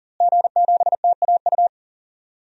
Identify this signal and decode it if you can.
Morse code, heard as a keyed tone, about 40 wpm, 700 Hz. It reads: G7TAU
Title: DXCC-100 Call Signs - 5 Characters - Repeated 2x - 40wpm